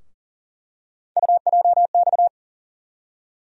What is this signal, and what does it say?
Morse code, heard as a keyed tone, about 40 wpm, 700 Hz. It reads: UJX
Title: USA Call Sign Postfixes - 3 Letters 40wpm